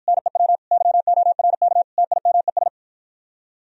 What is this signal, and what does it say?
Morse code, heard as a keyed tone, about 53 wpm, 700 Hz. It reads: NEW YORK TIMES